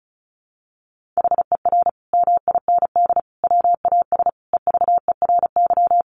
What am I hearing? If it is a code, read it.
Morse code, heard as a keyed tone, about 34 wpm, 700 Hz. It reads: HER MIND WAS EVERY
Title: Sentences from Top 1400 Words 35wpm